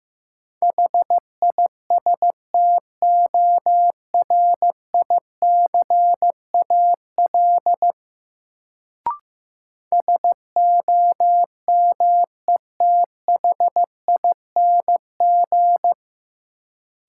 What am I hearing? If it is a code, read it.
Morse code, heard as a keyed tone, about 15 wpm, 700 Hz. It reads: HISTORICAL SOMETHING
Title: Sentences from Top 600 Words - New Word Drill - 15wpm